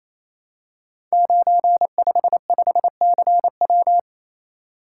Morse code, heard as a keyed tone, about 27 wpm, 700 Hz. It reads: 955CW
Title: Code Groups - 5 Letter:Numbers - 28wpm